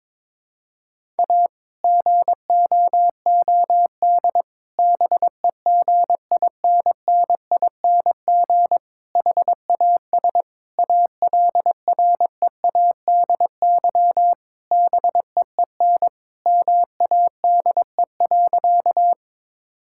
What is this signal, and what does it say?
Morse code, heard as a keyed tone, about 22 wpm, 700 Hz. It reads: A GOOD BEGINNING HAS ALREADY BEEN MADE.
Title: Sentences from Top 1000 Words - Review Part 5 - 22wpm